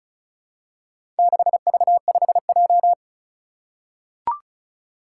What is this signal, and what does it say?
Morse code, heard as a keyed tone, about 36 wpm, 700 Hz. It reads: 6V5J
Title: Four Challenging Letter-Number Combinations 35wpm